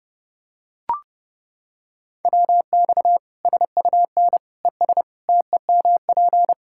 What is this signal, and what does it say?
Morse code, heard as a keyed tone, about 30 wpm, 700 Hz. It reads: WX SUN ES TEMP
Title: Practice QSO - Exchange 2 - Info - 30wpm